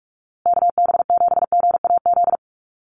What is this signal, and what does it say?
Morse code, heard as a keyed tone, about 44 wpm, 700 Hz. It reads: KB7GAZ
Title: USA Call Signs Repeated 2x 45wpm